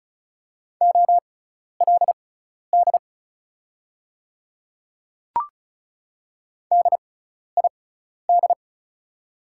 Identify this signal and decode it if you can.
Morse code, heard as a keyed tone, about 35 wpm, spaced as for 12 wpm, 700 Hz. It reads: OLD DID
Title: 6x Character Spacing - Top 200 Words - 35wpm